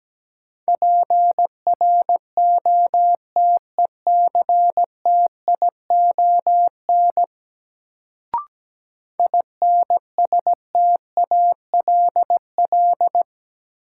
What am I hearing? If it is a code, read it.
Morse code, heard as a keyed tone, about 17 wpm, 700 Hz. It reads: PROTECTION INSTALL
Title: Sentences from Top 1600 Words - New Word Drill - 17wpm